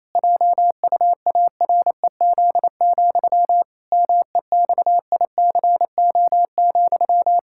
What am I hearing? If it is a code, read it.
Morse code, heard as a keyed tone, about 28 wpm, 700 Hz. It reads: JUAREZ, MEXICO,
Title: Pangram Challenge v2 - 28wpm